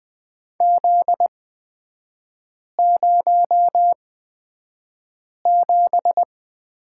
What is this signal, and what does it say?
Morse code, heard as a keyed tone, about 20 wpm, spaced as for 5 wpm, 700 Hz. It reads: Z07
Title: 243 Lesson 39e - Letter z - Three Characters 20wpm